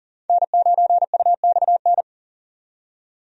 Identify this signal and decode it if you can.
Morse code, heard as a keyed tone, about 37 wpm, 700 Hz. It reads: N9UXN
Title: USA Call Signs - 5 Characters - 40wpm